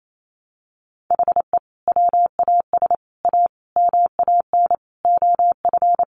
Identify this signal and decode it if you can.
Morse code, heard as a keyed tone, about 28 wpm, 700 Hz. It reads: HE WAS A MAN OF
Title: Sentences from Top 1200 Words 28wpm